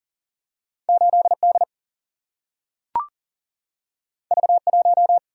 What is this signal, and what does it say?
Morse code, heard as a keyed tone, about 40 wpm, 700 Hz. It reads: 8D V1
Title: Single-Letter Single-Number 40wpm